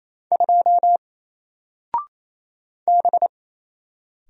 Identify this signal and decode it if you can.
Morse code, heard as a keyed tone, about 28 wpm, 700 Hz. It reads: 2 B
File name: Single-Letter-Number-28wpm